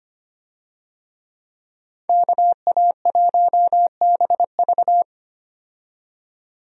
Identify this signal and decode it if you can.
Morse code, heard as a keyed tone, about 25 wpm, 700 Hz. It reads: KA1BV
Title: USA Call Signs - 5 Characters - 25wpm